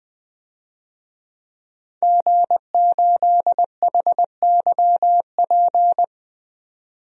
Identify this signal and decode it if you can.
Morse code, heard as a keyed tone, about 20 wpm, 700 Hz. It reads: G8HYP
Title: DXCC-100 Call Signs - 5 Characters - 20wpm